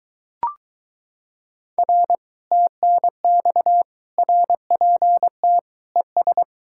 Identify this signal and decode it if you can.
Morse code, heard as a keyed tone, about 23 wpm, 700 Hz. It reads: R TNX RPT ES